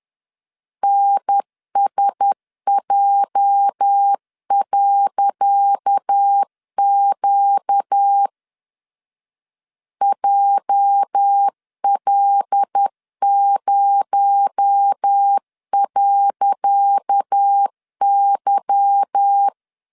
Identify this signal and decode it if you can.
Morse code, heard as a keyed tone, about 10 wpm, 800 Hz. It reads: NSJ.Q JL0.Y
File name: LEZIONI / 10 SIGNS / 10C DOT 10wpm 2940